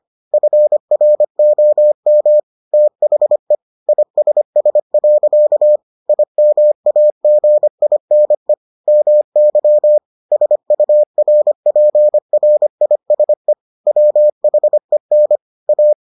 Morse code, heard as a keyed tone, about 25 wpm, 600 Hz. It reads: FROM THE ISS. IMAGINE MY SURPRISE WHEN A